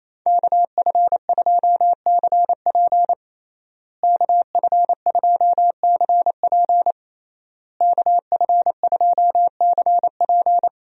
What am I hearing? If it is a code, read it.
Morse code, heard as a keyed tone, about 27 wpm, 700 Hz. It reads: KF2CP KF2CP KF2CP
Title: USA Call Signs - 5 Characters - Repeated 3x - 28wpm